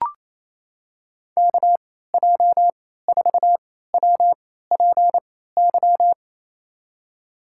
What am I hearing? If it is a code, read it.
Morse code, heard as a keyed tone, about 28 wpm, spaced as for 16 wpm, 700 Hz. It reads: KJ4WPY
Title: USA Call Signs - 6 Characters - 3x Character Spacing - 28wpm